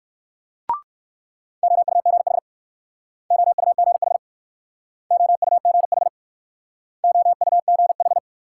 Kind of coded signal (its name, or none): Morse code